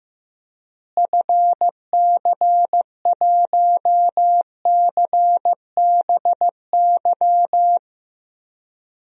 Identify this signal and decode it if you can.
Morse code, heard as a keyed tone, about 16 wpm, 700 Hz. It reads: FC1CBY
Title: DXCC-100 Call Signs - 6 Characters - Repeated 2x - 15wpm